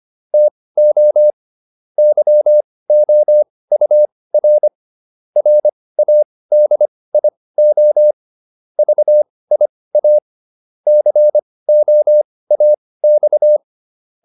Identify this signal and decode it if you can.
Morse code, heard as a keyed tone, about 25 wpm, spaced as for 18 wpm, 600 Hz. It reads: TO YOUR RADIO VIA COAX